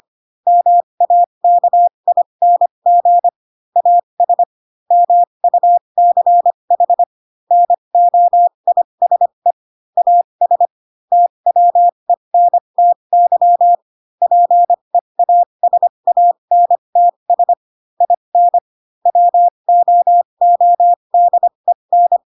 Morse code, heard as a keyed tone, about 25 wpm, spaced as for 22 wpm, 700 Hz. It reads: MAKING AS MUCH NOISE AS TWENTY PEASANTS IN WOODEN
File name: Pinocchio / Pinocchio ch3 18 25 ik1hge